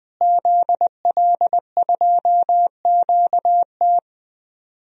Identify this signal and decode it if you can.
Morse code, heard as a keyed tone, about 20 wpm, 700 Hz. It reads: ZL2QT
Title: DXCC-100 Call Signs - 5 Characters - Repeated 3x - 20wpm